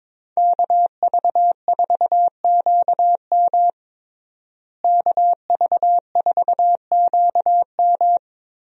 Morse code, heard as a keyed tone, about 21 wpm, 700 Hz. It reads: KV4QM KV4QM
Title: USA Call Signs - 5 Characters - Repeated 2x - 22wpm